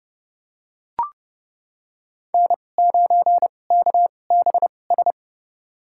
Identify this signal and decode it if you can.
Morse code, heard as a keyed tone, about 30 wpm, spaced as for 22 wpm, 700 Hz. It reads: N9KBS
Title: USA Call Signs - 2x Character Spacing - 30wpm